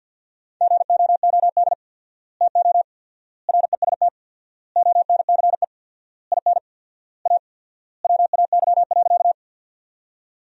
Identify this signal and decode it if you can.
Morse code, heard as a keyed tone, about 50 wpm, 700 Hz. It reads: GOOD TO REST ONCE IN A WAY.
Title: 4x Word-Spacing - Sentences from Top 500 Words - 50wpm